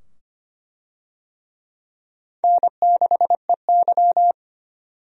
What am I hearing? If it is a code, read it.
Morse code, heard as a keyed tone, about 25 wpm, 700 Hz. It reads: N6EY
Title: USA Call Signs - 4 Characters - 25wpm